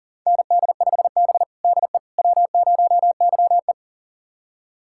Morse code, heard as a keyed tone, about 40 wpm, 700 Hz. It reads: ND5B DE W0YE
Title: USA Callsign DE Callsign - 4 Characters - 40wpm